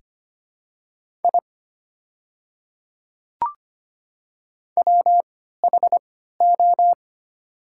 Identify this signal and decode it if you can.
Morse code, heard as a keyed tone, about 25 wpm, spaced as for 14 wpm, 690 Hz. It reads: I WHO
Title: 3x Character Spacing - Top 100 Words - 25wpm